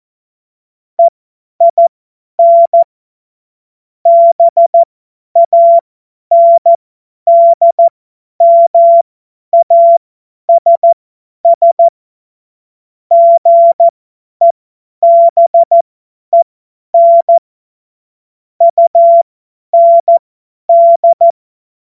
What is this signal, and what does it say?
Morse code, heard as a keyed tone, about 14 wpm, spaced as for 10 wpm, 680 Hz. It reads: EIN BANDMASS GEBEN UND
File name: Morsekurs / Witze / Witz 21. Juni 2022 50 F70